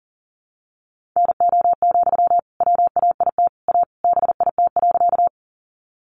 Morse code, heard as a keyed tone, about 40 wpm, 700 Hz. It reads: NO, WAIT A BIT.
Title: Sentences from Top 900 Words 40wpm